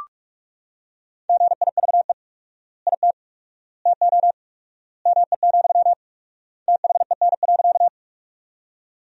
Morse code, heard as a keyed tone, about 43 wpm, 700 Hz. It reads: GIVE IT TO ME, THEN.
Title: 4x Word-Spacing - Sentences from Top 200 Words - 45wpm